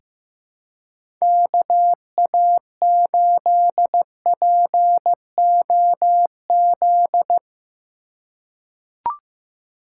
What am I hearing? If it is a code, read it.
Morse code, heard as a keyed tone, about 15 wpm, 700 Hz. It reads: KA8POZ
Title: K1USN SST QSO Practice - 15wpm